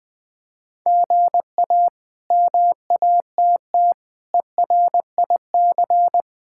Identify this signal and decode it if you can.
Morse code, heard as a keyed tone, about 20 wpm, 700 Hz. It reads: GA MATT ERIC
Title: K1USN SST QSO Practice - 20wpm